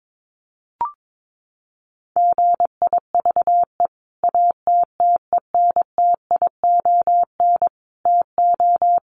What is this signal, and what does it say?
Morse code, heard as a keyed tone, about 22 wpm, 700 Hz. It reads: GIVE ATTENTION TO
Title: Sentences from Top 1000 Words 22wpm